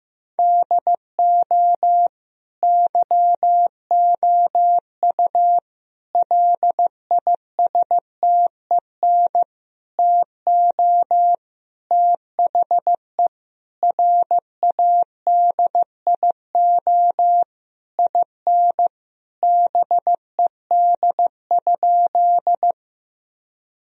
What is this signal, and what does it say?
Morse code, heard as a keyed tone, about 15 wpm, 700 Hz. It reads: DO YOU LISTEN TO THE RADIO IN BED?
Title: Sentences from Top 1000 Words - Review Part 3 - 15wpm